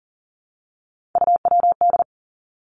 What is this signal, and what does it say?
Morse code, heard as a keyed tone, about 38 wpm, 700 Hz. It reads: UWD
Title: Three Challenging Letter-Number Combinations 40wpm